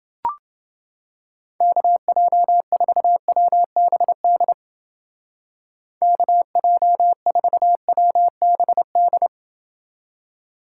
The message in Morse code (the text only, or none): KJ4WBD KJ4WBD